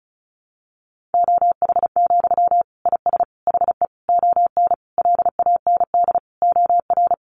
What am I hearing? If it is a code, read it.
Morse code, heard as a keyed tone, about 34 wpm, 700 Hz. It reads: OH, IS HE ON LAND OR
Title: Sentences from Top 500 Words 35wpm